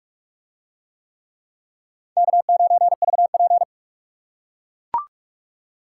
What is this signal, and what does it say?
Morse code, heard as a keyed tone, about 43 wpm, 700 Hz. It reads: K9VP